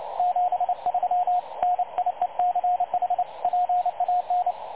Morse code, heard as a keyed tone, about 30 wpm, 720 Hz, with radio noise. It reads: 73 NIECH PAN
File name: dlazaawansowanych / nagraniazpasma / websdr recording 3565.0kHz